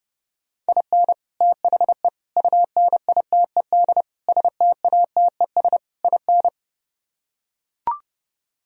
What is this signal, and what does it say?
Morse code, heard as a keyed tone, about 30 wpm, 700 Hz. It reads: IN THE UNITED STATES IN